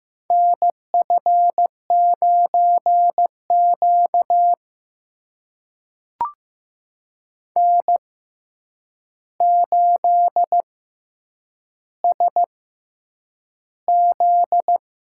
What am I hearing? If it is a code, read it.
Morse code, heard as a keyed tone, about 15 wpm, spaced as for 8 wpm, 700 Hz. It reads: NF9Q N8SZ
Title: USA Call Signs - 4 Characters - 6x Character Spacing - 15wpm